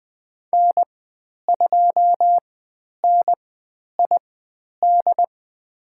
Morse code, heard as a keyed tone, about 20 wpm, spaced as for 10 wpm, 700 Hz. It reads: N2NID
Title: 109 Lesson 18g - Letter m - Call Signs 20wpm